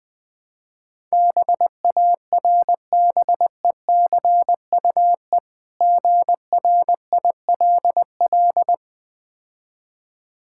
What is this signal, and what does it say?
Morse code, heard as a keyed tone, about 20 wpm, 700 Hz. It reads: BARBECUE GRILL
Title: Difficult Words - Speed-Racing with Hints - 20wpm